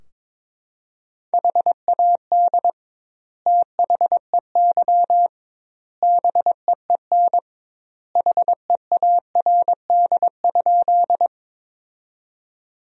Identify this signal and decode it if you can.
Morse code, heard as a keyed tone, about 23 wpm, 700 Hz. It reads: HAD THEY BEEN HEARD?
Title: 2x Word-Spacing - Sentences from Top 400 Words - 22wpm